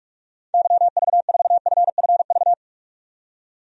Morse code, heard as a keyed tone, about 46 wpm, 700 Hz. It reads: YV4FFV
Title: DXCC-100 Call Signs - 6 Characters - 45wpm